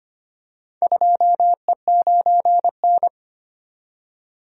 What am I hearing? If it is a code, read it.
Morse code, heard as a keyed tone, about 25 wpm, 700 Hz. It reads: 2E9N
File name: practice / 25wpm / Four Letter-Number Combinations - 25wpm